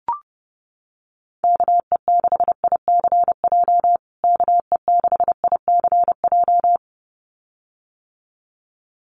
Morse code, heard as a keyed tone, about 30 wpm, 700 Hz. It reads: KE6ICJ KE6ICJ
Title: USA Call Signs Repeated 2x 30wpm